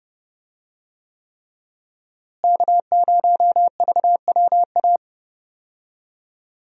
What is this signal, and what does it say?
Morse code, heard as a keyed tone, about 30 wpm, 700 Hz. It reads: K0VWA
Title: USA Call Signs - 5 Characters - Repeated 3x - 30wpm